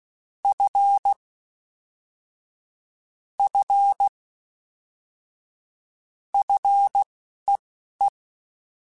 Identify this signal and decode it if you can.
Morse code, heard as a keyed tone, about 16 wpm, spaced as for 11 wpm, 800 Hz. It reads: F F FEE
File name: Fs / Single F mp3s / Single F Prefixes 3x2 Twice @10